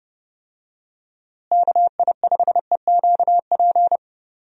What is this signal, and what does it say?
Morse code, heard as a keyed tone, about 30 wpm, 700 Hz. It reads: KI5EQP
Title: USA Call Signs - 6 Characters - Repeated 3x - 30wpm